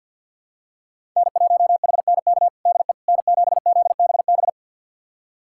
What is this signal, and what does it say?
Morse code, heard as a keyed tone, about 46 wpm, 700 Hz. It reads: N1HNK DE N7ZBB
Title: USA Callsign DE Callsign - 5 Characters - 50wpm